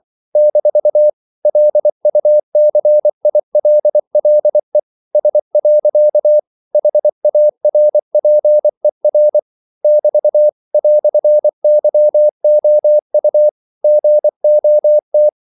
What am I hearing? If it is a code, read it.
Morse code, heard as a keyed tone, about 24 wpm, 600 Hz. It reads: - LUCILLE S. HARPER <BT> "YOU GOT
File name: CW / Practice / EnglishQuotations0324